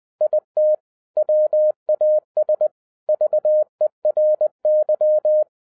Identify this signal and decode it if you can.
Morse code, heard as a keyed tone, about 20 wpm, 610 Hz. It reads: IT WAS VERY